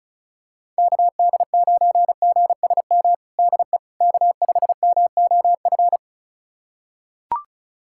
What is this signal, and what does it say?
Morse code, heard as a keyed tone, about 36 wpm, 700 Hz. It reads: KD9GSM DE K5MOF
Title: USA Callsign DE Callsign 35wpm